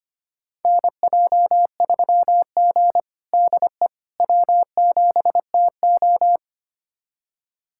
Morse code, heard as a keyed tone, about 25 wpm, 700 Hz. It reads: NJ3G DE W7TO
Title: USA Callsign DE Callsign - 4 Characters - 25wpm